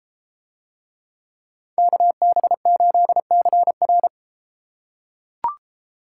Morse code, heard as a keyed tone, about 32 wpm, 700 Hz. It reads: KB8CR